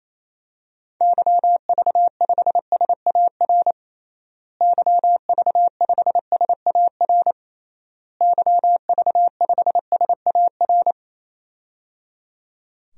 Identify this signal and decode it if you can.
Morse code, heard as a keyed tone, about 28 wpm, 700 Hz. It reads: YV5SAR YV5SAR YV5SAR
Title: DXCC-100 Call Signs - 6 Characters - Repeated 3x - 28wpm